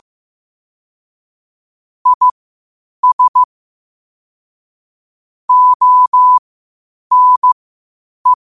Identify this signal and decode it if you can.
Morse code, heard as a keyed tone, about 15 wpm, spaced as for 8 wpm, 1000 Hz. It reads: IS ONE
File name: qotdfiles / 8wpm